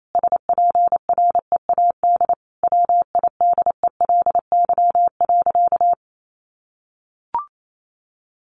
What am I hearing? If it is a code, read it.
Morse code, heard as a keyed tone, about 28 wpm, 700 Hz. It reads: SPREAD WIDELY.
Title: Sentences from Top 1200 Words 28wpm